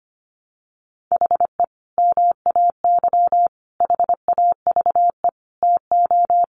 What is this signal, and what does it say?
Morse code, heard as a keyed tone, about 25 wpm, 700 Hz. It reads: HE MAY HAVE TO
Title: Sentences from Top 1100 Words 25wpm